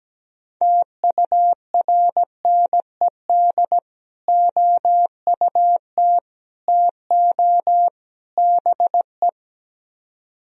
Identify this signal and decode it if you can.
Morse code, heard as a keyed tone, about 17 wpm, 700 Hz. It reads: TURNED OUT TO BE
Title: Speed-Racing - Sets of 4 Words 17wpm